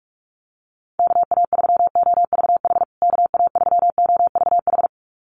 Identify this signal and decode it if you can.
Morse code, heard as a keyed tone, about 44 wpm, 700 Hz. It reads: KA3OVH KA3OVH
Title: USA Call Signs Repeated 2x 45wpm